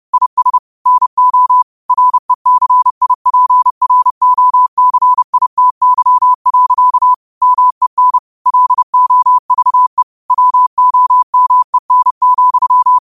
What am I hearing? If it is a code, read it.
Morse code, heard as a keyed tone, about 30 wpm, 1000 Hz. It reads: IS NO RECIPROCITY. MEN LOVE WOMEN,